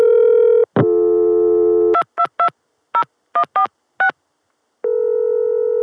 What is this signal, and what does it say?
DTMF, heard as telephone touch tones, tones of about 90 ms. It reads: 2220146